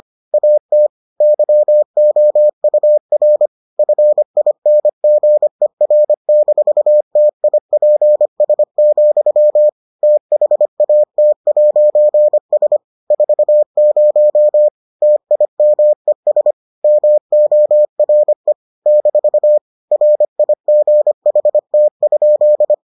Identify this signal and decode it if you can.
Morse code, heard as a keyed tone, about 25 wpm, 600 Hz. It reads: AT YOUR FINGER-TIPS, THAT'S 40 TIMES MORE - RIGHT?